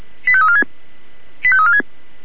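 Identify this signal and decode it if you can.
Five-tone selective call (ZVEI), heard as quick runs of five tones, tones of about 70 ms. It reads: 85346 85346
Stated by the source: ZVEI five-tone selective call received with SDR.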